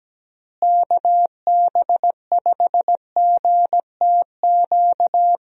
Morse code, heard as a keyed tone, about 17 wpm, 700 Hz. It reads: KB5GTQ
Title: USA Call Signs - 6 Characters - Repeated 2x - 17wpm